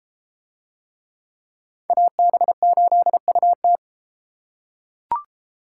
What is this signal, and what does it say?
Morse code, heard as a keyed tone, about 32 wpm, 700 Hz. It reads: AB8UT